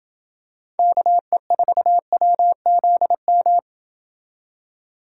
Morse code, heard as a keyed tone, about 27 wpm, 700 Hz. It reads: KE4WZM